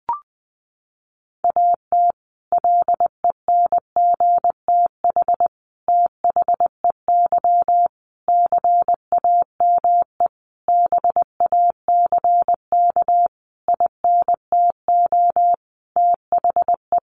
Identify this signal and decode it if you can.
Morse code, heard as a keyed tone, about 20 wpm, 700 Hz. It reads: AT LENGTH THEY CAME BACK INTO THE
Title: Sentences from Top 1000 Words 20wpm